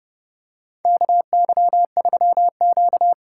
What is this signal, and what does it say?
Morse code, heard as a keyed tone, about 32 wpm, 700 Hz. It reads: KY3Q
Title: USA Call Signs - 4 Characters - 6x Character Spacing - 30wpm